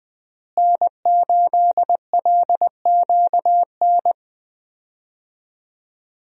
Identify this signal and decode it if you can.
Morse code, heard as a keyed tone, about 20 wpm, 700 Hz. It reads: N8LQN
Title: USA Call Signs - 5 Characters - Repeated 2x - 20wpm